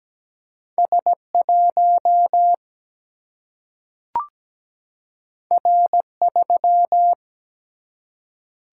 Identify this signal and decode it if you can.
Morse code, heard as a keyed tone, about 17 wpm, 700 Hz. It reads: S 1 R 3
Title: Single-Letter Single-Number 17wpm